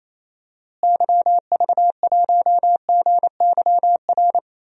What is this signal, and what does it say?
Morse code, heard as a keyed tone, about 28 wpm, 700 Hz. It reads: YV1GYR
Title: DXCC-100 Call Signs - 6 Characters - Repeated 2x - 28wpm